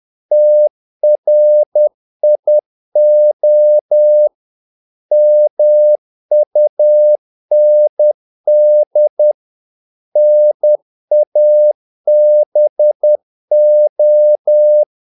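Morse code, heard as a keyed tone, about 10 wpm, 600 Hz. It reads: TRIO MUND NABO
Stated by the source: Morse practice of short Danish words at 10 wpm, from CW med Gnister.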